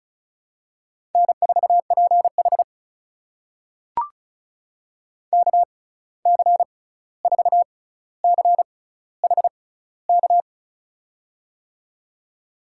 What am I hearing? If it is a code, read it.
Morse code, heard as a keyed tone, about 36 wpm, spaced as for 16 wpm, 700 Hz. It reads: N4PH KC4CHK
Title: USA Call Signs - 6x Character Spacing - 35wpm